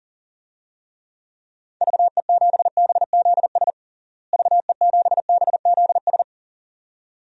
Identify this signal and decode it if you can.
Morse code, heard as a keyed tone, about 40 wpm, 700 Hz. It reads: VE7BZS VE7BZS
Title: DXCC-100 Call Signs - 6 Characters - Repeated 2x - 40wpm